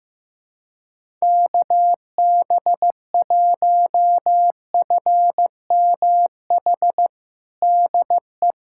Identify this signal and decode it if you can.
Morse code, heard as a keyed tone, about 15 wpm, 700 Hz. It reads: KB1FMH DE
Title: USA Callsign DE Callsign - 6 Characters - 15wpm